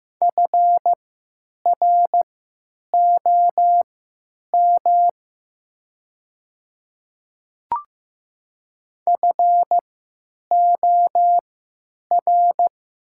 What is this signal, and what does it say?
Morse code, heard as a keyed tone, about 15 wpm, spaced as for 8 wpm, 700 Hz. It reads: FROM FOR
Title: 3x Character Spacing - Top 200 Words - 15wpm